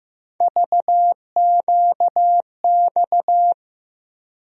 Morse code, heard as a keyed tone, about 15 wpm, 700 Hz. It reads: VQX
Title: USA Call Sign Postfixes - 3 Letters 15wpm